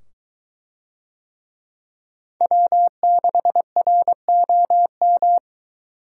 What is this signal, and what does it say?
Morse code, heard as a keyed tone, about 23 wpm, 700 Hz. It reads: W6ROM